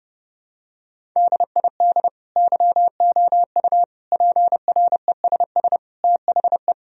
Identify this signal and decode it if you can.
Morse code, heard as a keyed tone, about 32 wpm, 700 Hz. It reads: DID YOU PRESS THE
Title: Sentences from Top 1000 Words - Review Part 1 - 30wpm